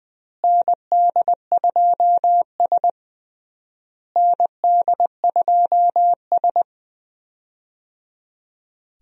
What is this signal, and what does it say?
Morse code, heard as a keyed tone, about 21 wpm, 700 Hz. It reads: ND2S ND2S
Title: USA Call Signs - 4 Characters - Repeated 3x - 20wpm